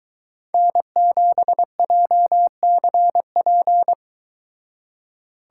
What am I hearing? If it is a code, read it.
Morse code, heard as a keyed tone, about 23 wpm, 700 Hz. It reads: N7JCP